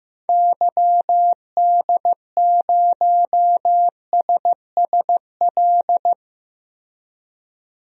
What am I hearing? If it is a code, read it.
Morse code, heard as a keyed tone, about 15 wpm, 700 Hz. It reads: YD0SSL
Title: DXCC-100 Call Signs - 6 Characters - Repeated 2x - 15wpm